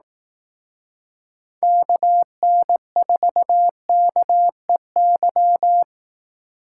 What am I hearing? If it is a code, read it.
Morse code, heard as a keyed tone, about 18 wpm, 700 Hz. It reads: KN4KEY
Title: Speed Racing - USA 2x3 Call Signs 15wpm